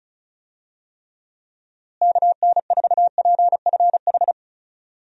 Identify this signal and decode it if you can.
Morse code, heard as a keyed tone, about 34 wpm, 700 Hz. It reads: KN4PFH
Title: USA Call Signs - 6 Characters - Repeated 3x - 35wpm